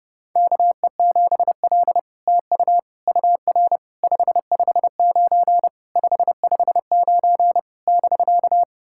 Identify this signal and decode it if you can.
Morse code, heard as a keyed tone, about 30 wpm, 700 Hz. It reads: KE7L TU UR 559 559 <BK>
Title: SOTA QSO Practice 30wpm